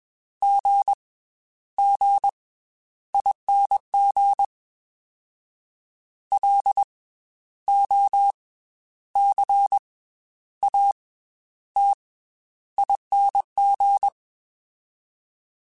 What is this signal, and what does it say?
Morse code, heard as a keyed tone, about 21 wpm, 800 Hz. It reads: G G ING L O C A T ING
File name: Special INGs / Special INGs-mp3s / Slow ing @20